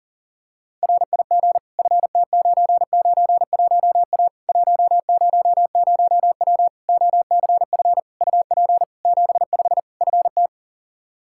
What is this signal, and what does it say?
Morse code, heard as a keyed tone, about 37 wpm, 700 Hz. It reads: RIG FT991A 100W OCF UP 75 FT
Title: K1USN SST QSO Practice 40wpm